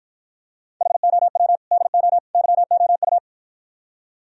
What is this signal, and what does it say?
Morse code, heard as a keyed tone, about 52 wpm, 700 Hz. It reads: HOW DO YOU